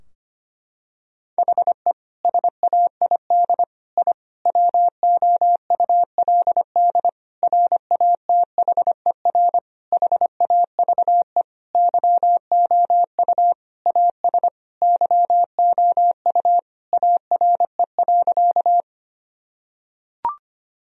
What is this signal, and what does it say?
Morse code, heard as a keyed tone, about 25 wpm, 700 Hz. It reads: HE SAID I WOULD RATHER HAVE YOU AS YOU ARE.
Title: Sentences from Top 1000 Words - Review Part 4 - 25wpm